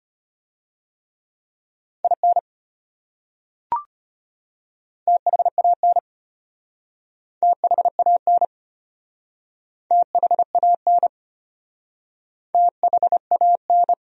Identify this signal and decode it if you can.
Morse code, heard as a keyed tone, about 33 wpm, 700 Hz. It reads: IN THAN THAN THAN THAN